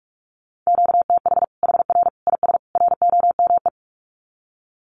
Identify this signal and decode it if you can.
Morse code, heard as a keyed tone, about 44 wpm, 700 Hz. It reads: QTH HR IS ROME
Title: Common QSO Phrases 45wpm